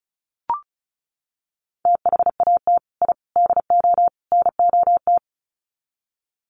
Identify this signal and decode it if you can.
Morse code, heard as a keyed tone, about 34 wpm, 700 Hz. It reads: THAT I DO NOT
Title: Sets of 4 Words 35wpm